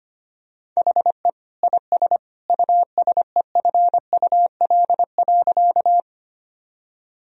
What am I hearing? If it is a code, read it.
Morse code, heard as a keyed tone, about 25 wpm, 700 Hz. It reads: HE IS USEFUL.
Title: Sentences from Top 1000 Words - Review Part 4 - 25wpm